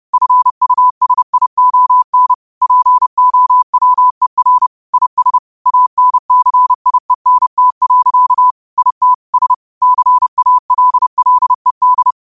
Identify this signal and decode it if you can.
Morse code, heard as a keyed tone, about 30 wpm, 1000 Hz. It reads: FUSION POWER IS ANCIENT. IT S CALLED
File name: qotdfiles / 30wpm